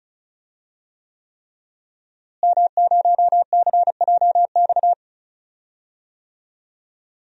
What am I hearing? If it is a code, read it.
Morse code, heard as a keyed tone, about 36 wpm, 700 Hz. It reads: M0CJX
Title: DXCC-100 Call Signs - 5 Characters - Repeated 3x - 35wpm